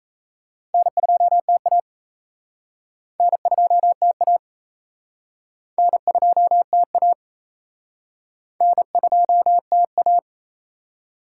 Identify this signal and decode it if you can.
Morse code, heard as a keyed tone, about 38 wpm, 700 Hz. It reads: N2TA N2TA N2TA N2TA